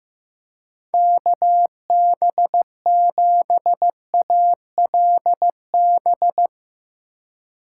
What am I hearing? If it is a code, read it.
Morse code, heard as a keyed tone, about 15 wpm, 700 Hz. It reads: KB7ALB
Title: USA Call Signs - 6 Characters - 2x Character Spacing - 15wpm